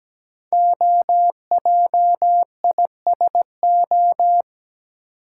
Morse code, heard as a keyed tone, about 17 wpm, 700 Hz. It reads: OJISO
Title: Code Groups - 5 Letters - 17wpm